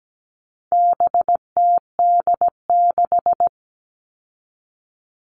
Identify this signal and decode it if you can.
Morse code, heard as a keyed tone, about 17 wpm, 700 Hz. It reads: BTD6
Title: All Cut Numbers - 4-Digit 17wpm